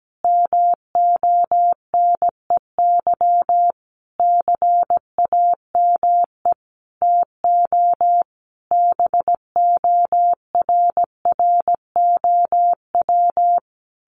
Morse code, heard as a keyed tone, about 17 wpm, 700 Hz. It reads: MONEY CAME TO BORROW
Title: Sentences from Top 1500 Words 17wpm